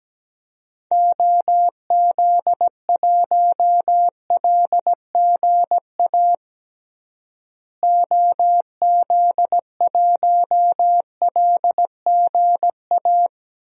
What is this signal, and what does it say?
Morse code, heard as a keyed tone, about 17 wpm, 700 Hz. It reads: OZ1LGA OZ1LGA
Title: DXCC-100 Call Signs - 6 Characters - Repeated 2x - 17wpm